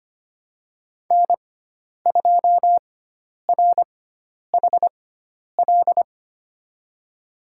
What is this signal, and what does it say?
Morse code, heard as a keyed tone, about 25 wpm, spaced as for 10 wpm, 700 Hz. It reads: N2RHL
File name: Lesson 16g - Letter c - Call Signs 25wpm